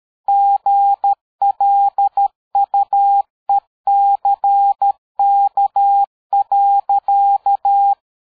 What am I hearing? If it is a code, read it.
Morse code, heard as a keyed tone, about 13 wpm, 800 Hz. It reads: GLUECK.
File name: dk5ke / PersWorte 60